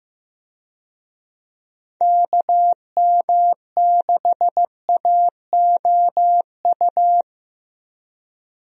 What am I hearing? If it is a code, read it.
Morse code, heard as a keyed tone, about 15 wpm, 700 Hz. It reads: KM6AOU
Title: USA Call Signs - 6 Characters - Repeated 3x - 15wpm